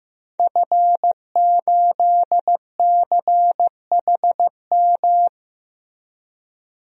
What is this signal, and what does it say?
Morse code, heard as a keyed tone, about 15 wpm, 700 Hz. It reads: F8CHM
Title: DXCC-100 Call Signs - 5 Characters - 15wpm